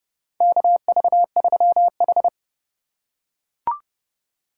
Morse code, heard as a keyed tone, about 30 wpm, 690 Hz. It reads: KV3H
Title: USA Call Signs - 4 Characters - 6x Character Spacing - 30wpm